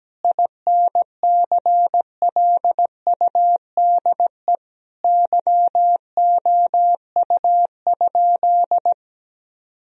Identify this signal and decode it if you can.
Morse code, heard as a keyed tone, about 17 wpm, 700 Hz. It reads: INCLUDE YOU?
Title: Sentences from Top 1000 Words - Review Part 3 - 17wpm